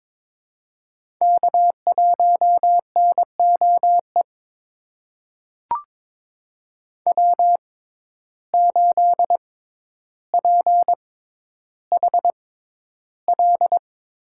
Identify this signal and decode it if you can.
Morse code, heard as a keyed tone, about 22 wpm, spaced as for 11 wpm, 700 Hz. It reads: K1NOE W8PHL
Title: USA Call Signs - 5 Characters - 6x Character Spacing - 22wpm